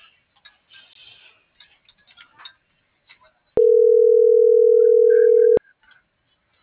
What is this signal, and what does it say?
Ringback tone (North America), 440 + 480 Hz.